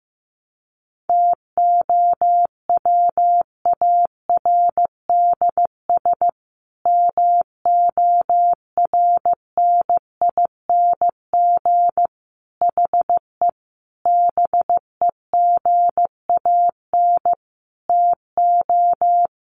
Morse code, heard as a keyed tone, about 14 wpm, 700 Hz. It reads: TOWARDS MORNING HE BEGAN TO
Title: Sentences from Top 1100 Words 15wpm